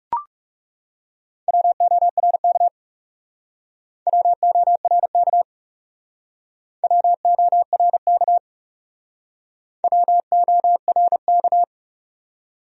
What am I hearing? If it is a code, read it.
Morse code, heard as a keyed tone, about 38 wpm, 700 Hz. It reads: WORK WORK WORK WORK